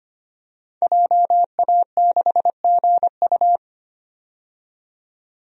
Morse code, heard as a keyed tone, about 25 wpm, 700 Hz. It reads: JA6GU
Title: DXCC-100 Call Signs - 5 Characters - 25wpm